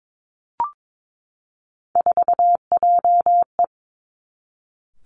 Morse code, heard as a keyed tone, about 22 wpm, 700 Hz. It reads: 4JE
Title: Three Challenging Letter-Number Combinations 22wpm